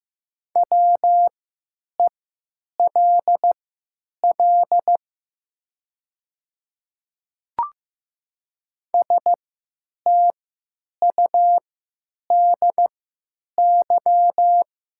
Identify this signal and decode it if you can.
Morse code, heard as a keyed tone, about 15 wpm, spaced as for 9 wpm, 690 Hz. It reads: WELL STUDY
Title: 3x Character Spacing - Top 300 Words - 15wpm